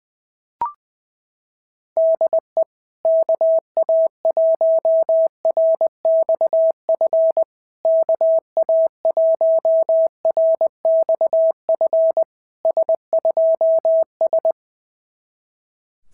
Morse code, heard as a keyed tone, about 21 wpm, 650 Hz. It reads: DE KA1RXF KA1RXF S2S
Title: SOTA QSO Practice 20wpm